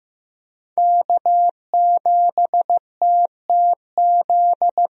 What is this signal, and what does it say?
Morse code, heard as a keyed tone, about 16 wpm, 700 Hz. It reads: K7TTZ
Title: USA Call Signs - 5 Characters - Repeated 3x - 15wpm